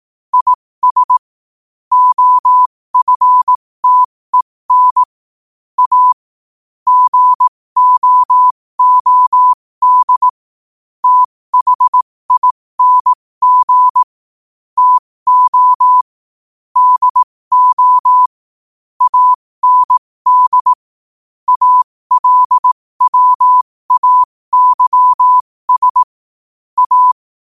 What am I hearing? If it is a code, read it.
Morse code, heard as a keyed tone, about 18 wpm, spaced as for 15 wpm, 1000 Hz. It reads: IS OFTEN A GOOD THING TO DO AND ALWAYS A
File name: qotdfiles / 15wpm